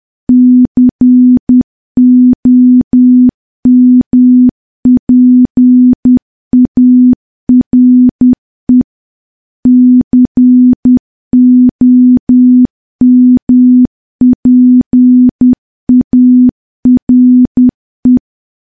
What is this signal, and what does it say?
Morse code, heard as a keyed tone, about 10 wpm, 250 Hz. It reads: COMPARE COMPARE
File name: morse / morse-training-10WPM